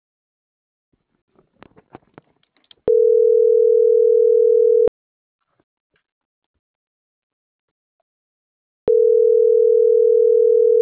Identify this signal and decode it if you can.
Ringback tone (North America), 440 + 480 Hz.